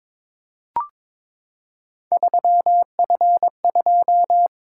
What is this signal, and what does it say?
Morse code, heard as a keyed tone, about 22 wpm, 700 Hz. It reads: 3F2
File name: practice / 22wpm / Three Letter-Number Combinations - 22wpm